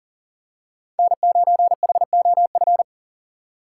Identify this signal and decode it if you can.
Morse code, heard as a keyed tone, about 40 wpm, 700 Hz. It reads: N9HOF
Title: USA Call Signs - 2x Character Spacing - 40wpm